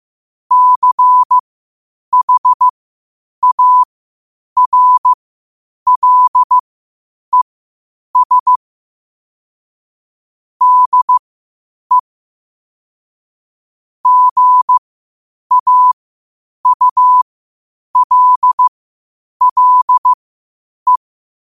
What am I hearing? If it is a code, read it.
Morse code, heard as a keyed tone, about 15 wpm, spaced as for 8 wpm, 1000 Hz. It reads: CHARLES DE GAULLE
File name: qotdfiles / 8wpm